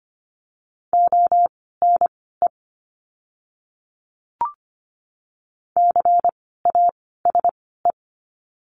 Morse code, heard as a keyed tone, about 25 wpm, spaced as for 16 wpm, 700 Hz. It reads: ONE CASE
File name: course / 25:10wpm / 094 Lesson 16f - Letter c - Words 25wpm